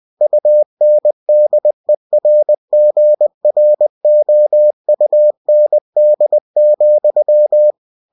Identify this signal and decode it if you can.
Morse code, heard as a keyed tone, about 20 wpm, 600 Hz. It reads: UNDERGROUND,
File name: cw-books / winnie-the-pooh / winnie-20wpm-page-0018